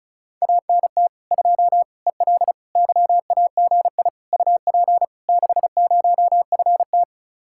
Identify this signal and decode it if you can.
Morse code, heard as a keyed tone, about 40 wpm, 700 Hz. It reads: ANT 2 EL YAGI UP 60FT